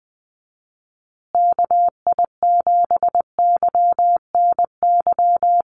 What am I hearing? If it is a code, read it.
Morse code, heard as a keyed tone, about 20 wpm, 700 Hz. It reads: KI7YNY
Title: USA Call Signs Repeated 3x 20wpm